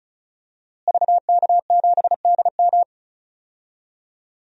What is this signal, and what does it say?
Morse code, heard as a keyed tone, about 36 wpm, 700 Hz. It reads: VK7DM
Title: DXCC-100 Call Signs - 5 Characters - Repeated 2x - 35wpm